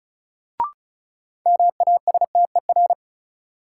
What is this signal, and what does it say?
Morse code, heard as a keyed tone, about 36 wpm, 700 Hz. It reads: MASTER
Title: Sentences from Top 1300 Words - New Word Drill - 35wpm